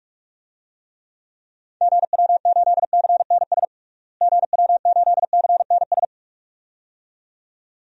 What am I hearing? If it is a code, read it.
Morse code, heard as a keyed tone, about 41 wpm, 700 Hz. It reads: GW8CNS GW8CNS
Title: DXCC-100 Call Signs - 6 Characters - Repeated 2x - 45wpm